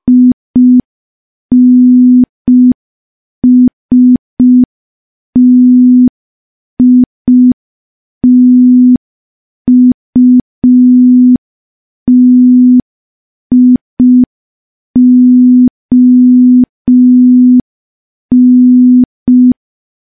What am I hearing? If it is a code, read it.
Morse code, heard as a keyed tone, about 5 wpm, 250 Hz. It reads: INSTITUTION
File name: morse / morse-training-05WPM